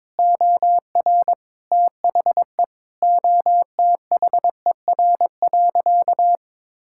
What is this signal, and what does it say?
Morse code, heard as a keyed tone, about 22 wpm, 700 Hz. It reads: OR THE OTHER.
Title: Sentences from Top 1000 Words - Review Part 6 - 22wpm